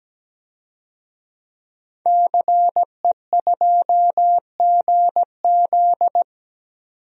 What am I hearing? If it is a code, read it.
Morse code, heard as a keyed tone, about 17 wpm, 700 Hz. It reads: CE2GZ
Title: DXCC-100 Call Signs - 5 Characters - 17wpm